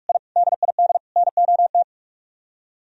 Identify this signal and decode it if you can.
Morse code, heard as a keyed tone, about 44 wpm, 700 Hz. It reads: I DID NOT